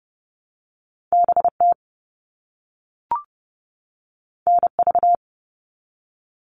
Morse code, heard as a keyed tone, about 30 wpm, 700 Hz. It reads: BT NV
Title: All Cut Numbers - 2-Digit 30wpm